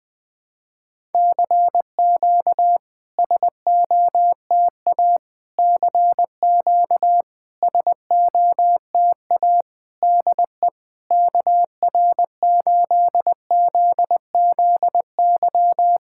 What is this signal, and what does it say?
Morse code, heard as a keyed tone, about 20 wpm, 700 Hz. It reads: CQ SOTA CQ SOTA DE KR8ZZY
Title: SOTA QSO Practice 20wpm